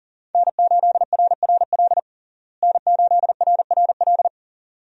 Morse code, heard as a keyed tone, about 37 wpm, 700 Hz. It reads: N8RRL N8RRL
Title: USA Call Signs - 5 Characters - Repeated 2x - 40wpm